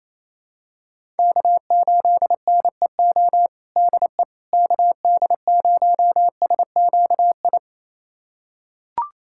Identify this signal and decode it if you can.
Morse code, heard as a keyed tone, about 28 wpm, 700 Hz. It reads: K8NEO DE KD0SQI
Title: USA Callsign DE Callsign 28wpm